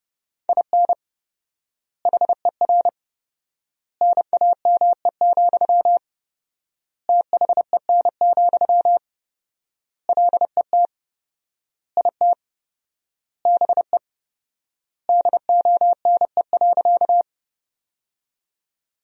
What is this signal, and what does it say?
Morse code, heard as a keyed tone, about 30 wpm, 700 Hz. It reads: IN HER NAME, THEN, LET IT BE DONE.
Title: 4x Word-Spacing - Sentences from Top 500 Words - 30wpm